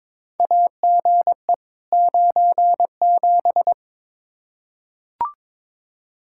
Morse code, heard as a keyed tone, about 22 wpm, 700 Hz. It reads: AGE 97
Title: Practice QSO - Exchange 3 - More Info - 22wpm